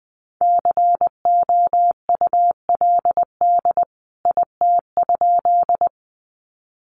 Morse code, heard as a keyed tone, about 20 wpm, 700 Hz. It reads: COULD IT?
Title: A Sound Of Thunder 20wpm